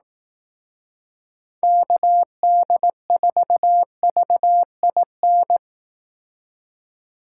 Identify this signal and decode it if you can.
Morse code, heard as a keyed tone, about 18 wpm, 700 Hz. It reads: KD4VIN
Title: Speed Racing - USA 2x3 Call Signs 15wpm